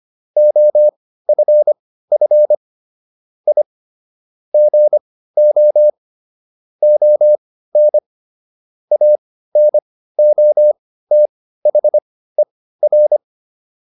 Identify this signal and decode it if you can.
Morse code, heard as a keyed tone, about 25 wpm, spaced as for 15 wpm, 600 Hz. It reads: OFF I GO ON ANOTHER